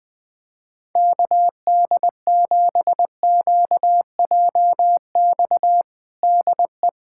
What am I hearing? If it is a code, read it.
Morse code, heard as a keyed tone, about 20 wpm, 700 Hz. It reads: KD7QJX DE
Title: USA Callsign DE Callsign 20wpm